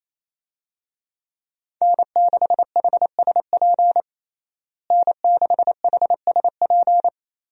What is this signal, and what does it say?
Morse code, heard as a keyed tone, about 28 wpm, 700 Hz. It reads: N6HSP N6HSP
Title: USA Call Signs - 5 Characters - Repeated 2x - 28wpm